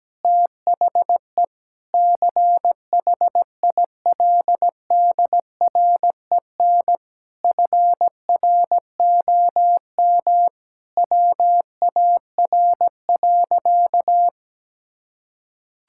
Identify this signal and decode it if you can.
Morse code, heard as a keyed tone, about 17 wpm, 700 Hz. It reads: THE CHILDREN FROM WAR.
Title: Sentences from Top 1000 Words - Review Part 6 - 17wpm